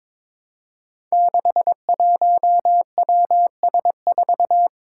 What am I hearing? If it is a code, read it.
Morse code, heard as a keyed tone, about 22 wpm, 700 Hz. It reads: 61WS4
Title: Five Challenging Letter-Number Combinations 22wpm